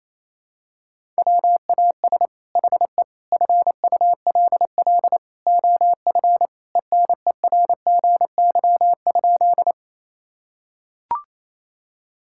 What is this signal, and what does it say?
Morse code, heard as a keyed tone, about 28 wpm, 700 Hz. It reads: WAS HE FULL OF ENERGY?
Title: Sentences from Top 1000 Words - Review Part 6 - 28wpm